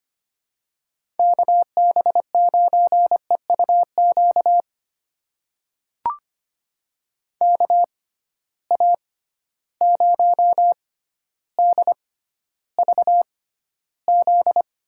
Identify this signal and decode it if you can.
Morse code, heard as a keyed tone, about 25 wpm, spaced as for 13 wpm, 700 Hz. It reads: KB9EUQ KA0DVZ
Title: USA Call Signs - 6 Characters - 6x Character Spacing - 25wpm